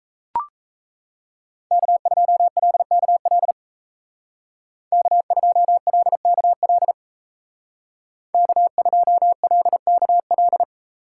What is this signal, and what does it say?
Morse code, heard as a keyed tone, about 36 wpm, 700 Hz. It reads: K2LKL K2LKL K2LKL